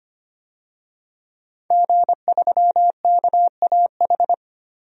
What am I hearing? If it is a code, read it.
Morse code, heard as a keyed tone, about 25 wpm, 700 Hz. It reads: G3KAH
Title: DXCC-100 Call Signs - 5 Characters - Repeated 2x - 25wpm